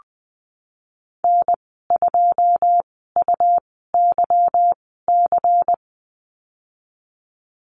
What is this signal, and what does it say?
Morse code, heard as a keyed tone, about 20 wpm, spaced as for 15 wpm, 700 Hz. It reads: N2UYC
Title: 193 Lesson 31g - Letter v - Call Signs 20wpm